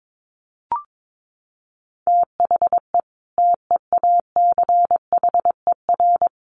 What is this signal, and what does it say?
Morse code, heard as a keyed tone, about 22 wpm, 700 Hz. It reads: THE TEACHER
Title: Additional Easy Phrases 22wpm